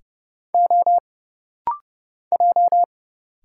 Morse code, heard as a keyed tone, about 30 wpm, 700 Hz. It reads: O J
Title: Single Letter-Number - Rapid-Fire 30wpm